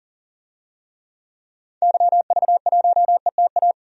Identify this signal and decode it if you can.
Morse code, heard as a keyed tone, about 40 wpm, 700 Hz. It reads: YV1ETA
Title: DXCC-100 Call Signs - 6 Characters - 40wpm